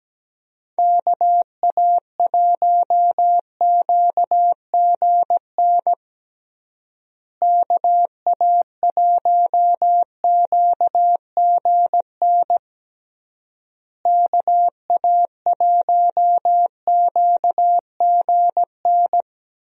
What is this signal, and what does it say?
Morse code, heard as a keyed tone, about 17 wpm, 700 Hz. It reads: KA1QGN KA1QGN KA1QGN
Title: USA Call Signs - 6 Characters - Repeated 3x - 17wpm